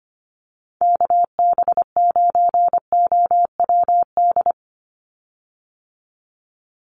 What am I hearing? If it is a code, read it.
Morse code, heard as a keyed tone, about 25 wpm, 700 Hz. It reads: KB9OWD
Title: Call Signs - 25wpm